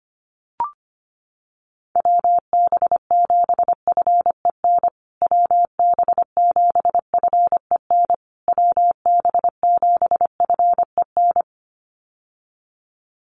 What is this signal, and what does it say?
Morse code, heard as a keyed tone, about 25 wpm, 700 Hz. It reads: WB7FEN WB7FEN WB7FEN
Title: USA Call Signs Repeated 3x 25wpm